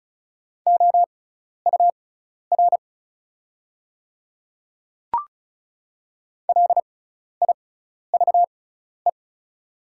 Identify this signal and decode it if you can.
Morse code, heard as a keyed tone, about 35 wpm, spaced as for 12 wpm, 690 Hz. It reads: OUR LIVE
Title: 6x Character Spacing - Top 200 Words - 35wpm